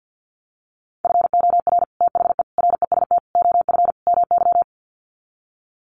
Morse code, heard as a keyed tone, about 48 wpm, 700 Hz. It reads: FOR THE REST OF MY
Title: Sets of 5 Words 50wpm